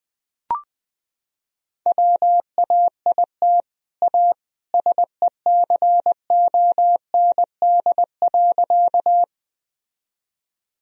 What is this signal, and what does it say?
Morse code, heard as a keyed tone, about 20 wpm, 700 Hz. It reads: WAIT A SECOND.
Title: Sentences from Top 1000 Words - Review Part 1 - 20wpm